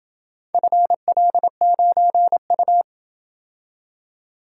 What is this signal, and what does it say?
Morse code, heard as a keyed tone, about 28 wpm, 700 Hz. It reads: FL9U